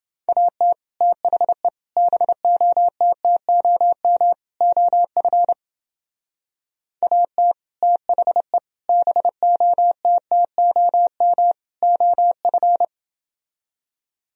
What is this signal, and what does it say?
Morse code, heard as a keyed tone, about 29 wpm, 700 Hz. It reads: AT THE BOTTOM OF AT THE BOTTOM OF